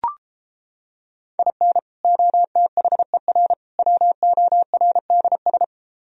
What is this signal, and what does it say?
Morse code, heard as a keyed tone, about 29 wpm, 700 Hz. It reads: IN OTHER WORDS